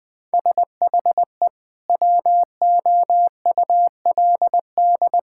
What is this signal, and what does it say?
Morse code, heard as a keyed tone, about 20 wpm, 700 Hz. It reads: SHE WOULD
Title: Sentences from Top 1600 Words 20wpm